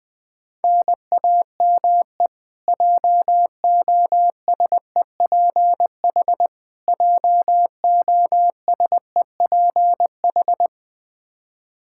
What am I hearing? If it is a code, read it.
Morse code, heard as a keyed tone, about 20 wpm, 700 Hz. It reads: NAME JOSEPH JOSEPH
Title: K1USN SST QSO Practice 20wpm